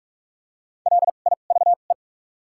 Morse code, heard as a keyed tone, about 46 wpm, spaced as for 33 wpm, 700 Hz. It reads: LIVE
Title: 2x Character Spacing - Top 200 Words - 45wpm